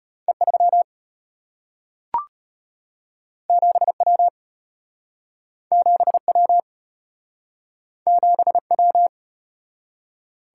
Morse code, heard as a keyed tone, about 34 wpm, 700 Hz. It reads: E3 7W 7W 7W